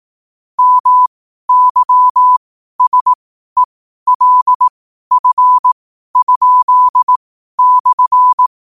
Morse code, heard as a keyed tone, about 18 wpm, spaced as for 12 wpm, 1000 Hz. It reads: MYSELF?/
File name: qotdfiles / 12wpm